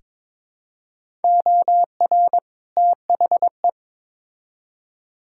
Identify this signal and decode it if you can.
Morse code, heard as a keyed tone, about 22 wpm, 700 Hz. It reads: OR THE
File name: Build A Phrase - 2-Word Phrases - 22wpm